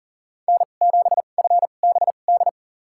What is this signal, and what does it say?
Morse code, heard as a keyed tone, about 40 wpm, spaced as for 29 wpm, 700 Hz. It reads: N7FBD
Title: USA Call Signs - 5 Characters - 2x Character Spacing - 40wpm